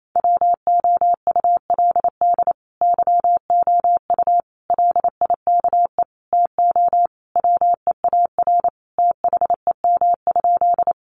Morse code, heard as a keyed tone, about 28 wpm, 700 Hz. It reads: WOULD YOU LIKE TO WEAR THEM?
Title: Sentences from Top 1500 Words 28wpm